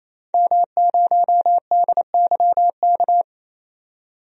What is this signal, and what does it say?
Morse code, heard as a keyed tone, about 28 wpm, 700 Hz. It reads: M0DYK
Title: DXCC-100 Call Signs - 5 Characters - 28wpm